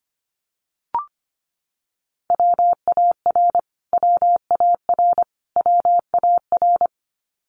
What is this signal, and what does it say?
Morse code, heard as a keyed tone, about 25 wpm, 700 Hz. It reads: WAR WAR WAR
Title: Top 500 Words Repeated 25wpm